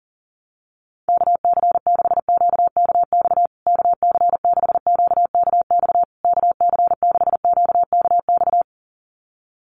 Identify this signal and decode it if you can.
Morse code, heard as a keyed tone, about 40 wpm, 700 Hz. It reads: KC6QKX KC6QKX KC6QKX
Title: USA Call Signs Repeated 3x 40wpm